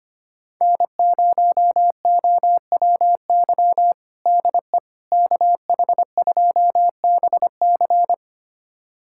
Morse code, heard as a keyed tone, about 25 wpm, 700 Hz. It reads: N0OWY DE KH2BC
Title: USA Callsign DE Callsign - 5 Characters - 25wpm